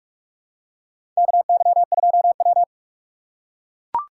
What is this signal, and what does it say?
Morse code, heard as a keyed tone, about 44 wpm, 700 Hz. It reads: KY2W
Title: USA Call Signs - 4 Characters - 6x Character Spacing - 45wpm